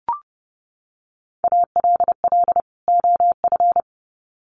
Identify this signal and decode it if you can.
Morse code, heard as a keyed tone, about 30 wpm, 700 Hz. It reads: ALL OF
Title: Sets of 2 Words 30wpm